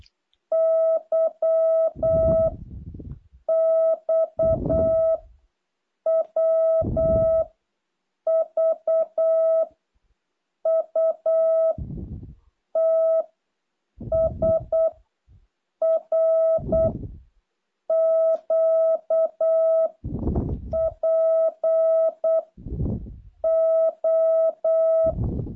Morse code, heard as a keyed tone, about 8 wpm, 660 Hz. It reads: Y X W V U T S R Q P O
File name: jh8cbh / 2023cw